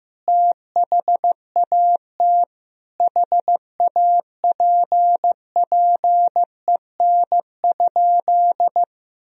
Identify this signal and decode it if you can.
Morse code, heard as a keyed tone, about 16 wpm, 700 Hz. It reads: THAT HAPPEN?
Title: Sentences from Top 400 Words and Abbreviations 15wpm